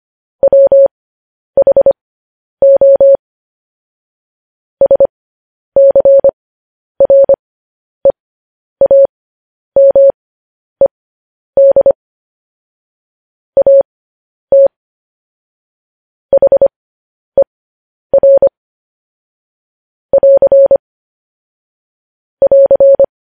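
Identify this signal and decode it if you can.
Morse code, heard as a keyed tone, about 25 wpm, spaced as for 10 wpm, 550 Hz. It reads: WHO SCREAMED AT HER <AR> <AR>
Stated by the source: Morse code transcription of news headlines.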